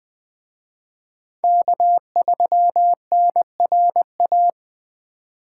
Morse code, heard as a keyed tone, about 23 wpm, 700 Hz. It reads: K3NRA
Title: Speed Racing - USA 1x3 Call Signs 20wpm